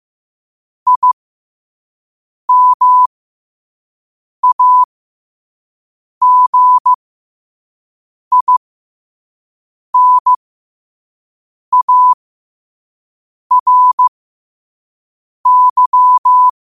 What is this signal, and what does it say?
Morse code, heard as a keyed tone, about 15 wpm, spaced as for 5 wpm, 1000 Hz. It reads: IMAGINARY
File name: qotdfiles / 5wpm